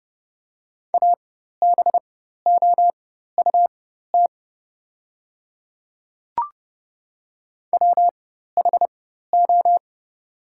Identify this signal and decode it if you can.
Morse code, heard as a keyed tone, about 32 wpm, spaced as for 14 wpm, 700 Hz. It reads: ABOUT WHO
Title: 4x Character Spacing - Top 100 Words - 30wpm